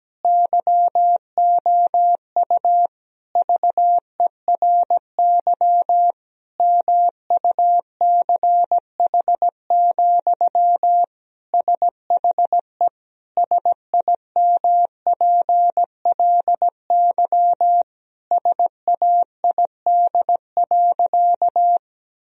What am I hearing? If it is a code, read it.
Morse code, heard as a keyed tone, about 17 wpm, 700 Hz. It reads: YOU VERY MUCH, SHE SIMPLY SAID.
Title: Sentences from Top 1000 Words - Review Part 4 - 17wpm